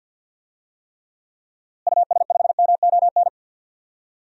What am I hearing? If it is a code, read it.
Morse code, heard as a keyed tone, about 46 wpm, 700 Hz. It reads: US5MON
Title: DXCC-100 Call Signs - 6 Characters - Repeated 3x - 50wpm